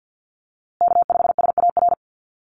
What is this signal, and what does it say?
Morse code, heard as a keyed tone, about 50 wpm, 700 Hz. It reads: K5SAR
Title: USA Call Signs Repeated 3x 50wpm